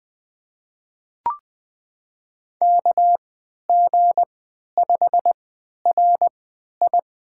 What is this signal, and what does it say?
Morse code, heard as a keyed tone, about 20 wpm, spaced as for 11 wpm, 700 Hz. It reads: KG5RI
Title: USA Call Signs - 3x Character Spacing - 20wpm